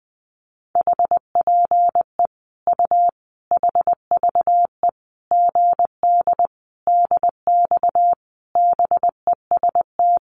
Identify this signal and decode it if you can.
Morse code, heard as a keyed tone, about 20 wpm, 700 Hz. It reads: HPE U HVE GD DX BEST
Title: Common QSO Phrases 20wpm